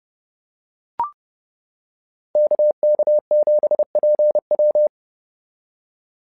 Morse code, heard as a keyed tone, about 30 wpm, 600 Hz. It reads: KK7PW
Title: SOTA QSO Practice 30wpm